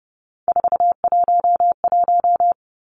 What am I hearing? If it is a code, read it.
Morse code, heard as a keyed tone, about 30 wpm, 700 Hz. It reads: 411
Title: Numbers - 3-Digit 30wpm